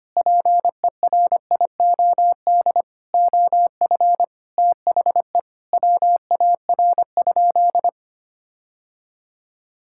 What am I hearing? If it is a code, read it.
Morse code, heard as a keyed tone, about 25 wpm, 700 Hz. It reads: PERIOD OF THE WAR?
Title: Sentences from Top 1000 Words - Review Part 6 - 25wpm